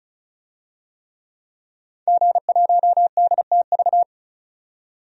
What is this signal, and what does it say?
Morse code, heard as a keyed tone, about 34 wpm, 700 Hz. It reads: G1DTV
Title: DXCC-100 Call Signs - 5 Characters - 35wpm